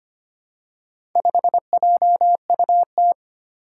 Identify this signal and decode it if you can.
Morse code, heard as a keyed tone, about 25 wpm, 700 Hz. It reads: 5JUT
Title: Four Challenging Letter-Number Combinations 25wpm